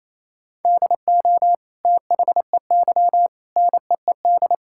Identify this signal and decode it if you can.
Morse code, heard as a keyed tone, about 27 wpm, 700 Hz. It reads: DO THEY NEED
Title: Sentences from Top 1000 Words - Review Part 5 - 28wpm